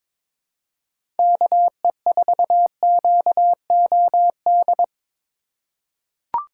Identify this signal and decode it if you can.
Morse code, heard as a keyed tone, about 22 wpm, 700 Hz. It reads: KE4QOD
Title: USA Call Signs - 6 Characters - 3x Character Spacing - 22wpm